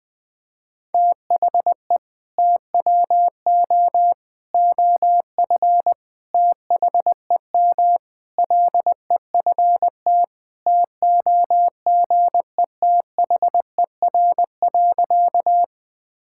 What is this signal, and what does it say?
Morse code, heard as a keyed tone, about 21 wpm, 700 Hz. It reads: THE TWO OF THEM LEFT TOGETHER.
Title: Sentences with 4-Word Phrases - 20wpm